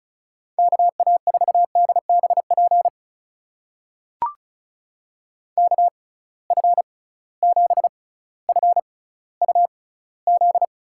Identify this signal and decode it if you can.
Morse code, heard as a keyed tone, about 35 wpm, spaced as for 18 wpm, 700 Hz. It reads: KA4DBP KF7FUZ
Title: USA Call Signs - 6 Characters - 6x Character Spacing - 35wpm